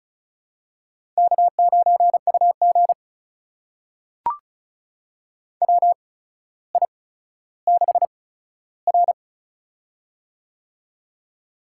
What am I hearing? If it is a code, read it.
Morse code, heard as a keyed tone, about 36 wpm, spaced as for 15 wpm, 700 Hz. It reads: K9UG WI6R
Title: USA Call Signs - 4 Characters - 8x Character Spacing - 35wpm